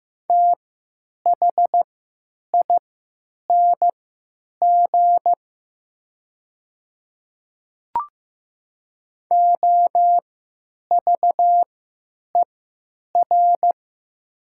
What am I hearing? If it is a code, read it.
Morse code, heard as a keyed tone, about 15 wpm, spaced as for 9 wpm, 700 Hz. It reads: THING OVER
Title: 3x Character Spacing - Top 100 Words - 15wpm